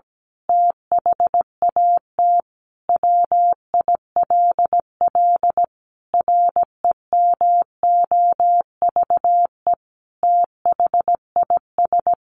Morse code, heard as a keyed tone, about 17 wpm, 700 Hz. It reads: THAT WILL REMOVE THIS
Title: Sentences from Top 1500 Words 17wpm